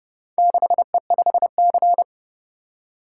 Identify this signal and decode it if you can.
Morse code, heard as a keyed tone, about 29 wpm, 700 Hz. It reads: 6E5C